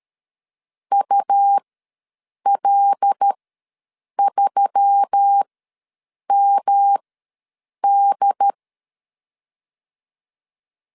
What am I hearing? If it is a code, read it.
Morse code, heard as a keyed tone, about 13 wpm, spaced as for 7 wpm, 800 Hz. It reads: UL3MD
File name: LEZIONI / 5 LBVC / 5 L ws 12wpm 2958